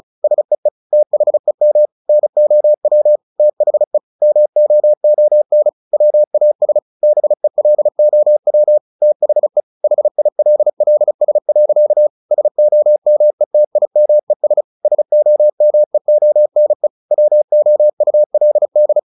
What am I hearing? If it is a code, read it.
Morse code, heard as a keyed tone, about 34 wpm, 600 Hz. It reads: SEE THEM NOW THE MOON WAS BELOW THE HILLS. SOMETIMES SOMEONE WOULD
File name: the old man and the sea / the old man and the sea 1 35wpm